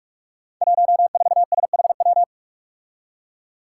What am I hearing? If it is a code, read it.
Morse code, heard as a keyed tone, about 46 wpm, 700 Hz. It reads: 14SHW
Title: Five Challenging Letter-Number Combinations 45wpm